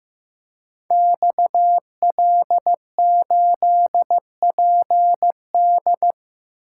Morse code, heard as a keyed tone, about 15 wpm, 700 Hz. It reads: XL8PD
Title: Code Groups - 5 Letter:Numbers - 15wpm